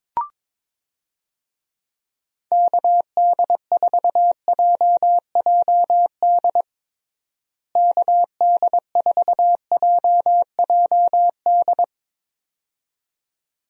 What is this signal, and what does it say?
Morse code, heard as a keyed tone, about 22 wpm, 700 Hz. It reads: KD4JJD KD4JJD
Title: USA Call Signs - 6 Characters - Repeated 2x - 22wpm